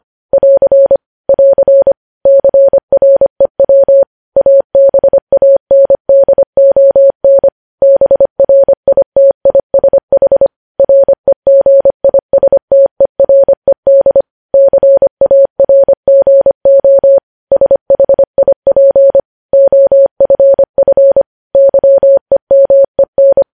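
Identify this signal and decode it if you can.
Morse code, heard as a keyed tone, about 25 wpm, 550 Hz. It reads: <AR> <AR> CREW ABANDON BRITISH REGISTERED CARGO SHIP OFF YEMEN
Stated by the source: Morse code news headlines sent at 25 wpm.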